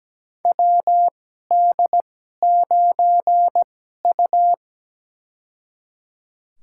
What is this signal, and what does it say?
Morse code, heard as a keyed tone, about 17 wpm, spaced as for 12 wpm, 700 Hz. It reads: WD9U
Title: USA Call Signs - 2x Character Spacing - 17wpm